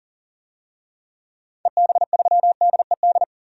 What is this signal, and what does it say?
Morse code, heard as a keyed tone, about 40 wpm, 700 Hz. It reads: EB3DED
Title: DXCC-100 Call Signs - 6 Characters - 40wpm